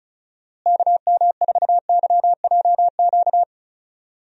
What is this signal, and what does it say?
Morse code, heard as a keyed tone, about 34 wpm, 700 Hz. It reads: KM4YJQ
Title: USA Call Signs - 6 Characters - Repeated 3x - 35wpm